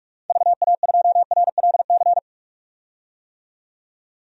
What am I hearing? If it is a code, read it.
Morse code, heard as a keyed tone, about 47 wpm, 690 Hz. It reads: VA2RLC
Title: DXCC-100 Call Signs - 6 Characters - 45wpm